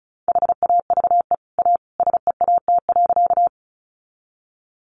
Morse code, heard as a keyed tone, about 34 wpm, 700 Hz. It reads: HAVE A SEAT.
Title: Sentences from Top 1500 Words 35wpm